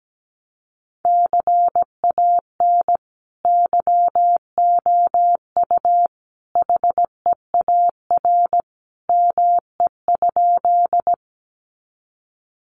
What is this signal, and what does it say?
Morse code, heard as a keyed tone, about 17 wpm, 700 Hz. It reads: CAN YOU HEAR ME?
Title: Sentences from Top 300 Words 17wpm